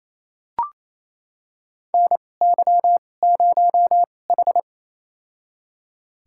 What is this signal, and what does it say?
Morse code, heard as a keyed tone, about 27 wpm, spaced as for 20 wpm, 700 Hz. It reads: NY0H
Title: USA Call Signs - 4 Characters - 2x Character Spacing - 28wpm